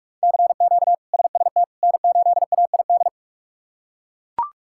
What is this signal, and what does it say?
Morse code, heard as a keyed tone, about 41 wpm, 700 Hz. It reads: CQ SST N8AID
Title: K1USN SST QSO Practice - 45wpm